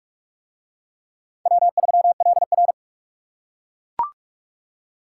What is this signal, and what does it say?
Morse code, heard as a keyed tone, about 42 wpm, 700 Hz. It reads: W3LR